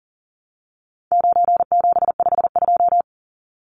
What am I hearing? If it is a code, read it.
Morse code, heard as a keyed tone, about 40 wpm, 700 Hz. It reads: 9752
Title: Numbers - 4-Digit 40wpm